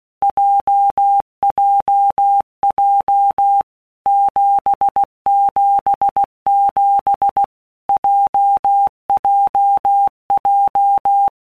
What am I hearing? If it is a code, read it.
Morse code, heard as a keyed tone, about 16 wpm, 800 Hz. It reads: JJJ 777 JJJ
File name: Basic Letter Groups By Blocks / Groups@15 / JZ17@15